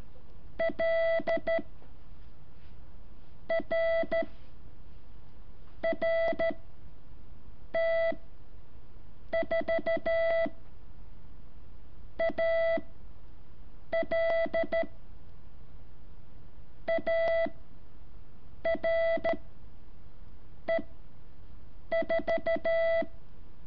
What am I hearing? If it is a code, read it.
Morse code, heard as a keyed tone, about 13 wpm, 660 Hz. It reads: L R R T 4 A L A R E 4
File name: curso / ejercicio 7